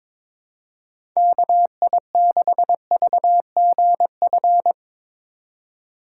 Morse code, heard as a keyed tone, about 24 wpm, 700 Hz. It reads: KI6VGF
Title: Speed Racing - USA 2x3 Call Signs 22wpm